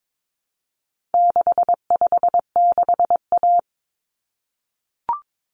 Morse code, heard as a keyed tone, about 22 wpm, 700 Hz. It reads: 656A
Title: Common Cut Numbers - 4-Digit 22wpm